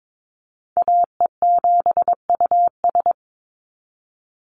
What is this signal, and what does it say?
Morse code, heard as a keyed tone, about 22 wpm, 700 Hz. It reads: AE7US
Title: Call Signs Repeated 3x - 22wpm